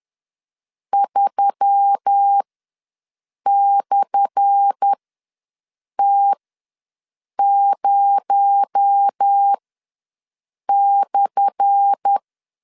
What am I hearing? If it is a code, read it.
Morse code, heard as a keyed tone, about 10 wpm, spaced as for 6 wpm, 800 Hz. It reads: 3/T0/
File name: LEZIONI / 10 SIGNS / 10A SLASH ws 10wpm 4121